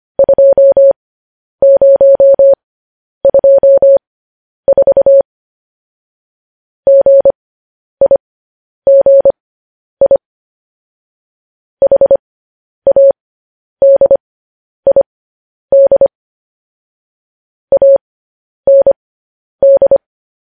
Morse code, heard as a keyed tone, about 25 wpm, spaced as for 10 wpm, 550 Hz. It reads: 2024 GIGI HADID AND
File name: podcast / News headlines-1x-25x10